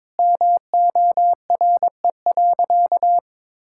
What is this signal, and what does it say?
Morse code, heard as a keyed tone, about 22 wpm, 700 Hz. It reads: MORE.
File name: practice / 22wpm / Sentences with 2-Word Phrases - 22wpm